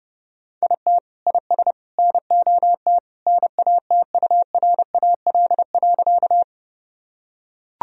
Morse code, heard as a keyed tone, about 32 wpm, 700 Hz. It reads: IT IS NOT NATURAL.
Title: Sentences from Top 1000 Words - Review Part 4 - 30wpm